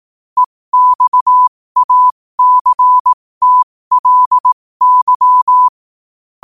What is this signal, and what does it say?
Morse code, heard as a keyed tone, about 18 wpm, spaced as for 15 wpm, 1000 Hz. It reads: EXACTLY
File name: qotdfiles / 15wpm